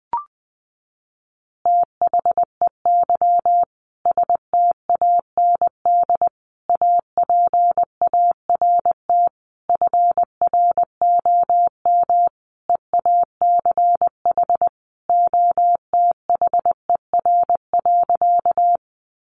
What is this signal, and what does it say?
Morse code, heard as a keyed tone, about 20 wpm, 700 Hz. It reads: THEY STAND APART FROM EACH OTHER.
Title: Easy Phrases 20wpm